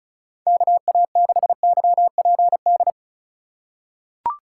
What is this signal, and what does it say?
Morse code, heard as a keyed tone, about 36 wpm, 700 Hz. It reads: KA6YPD
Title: USA Call Signs - 3x Character Spacing - 35wpm